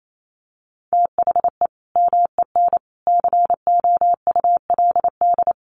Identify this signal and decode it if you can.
Morse code, heard as a keyed tone, about 28 wpm, 700 Hz. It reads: THE MEN COULD
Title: Sentences from Top 1200 Words 28wpm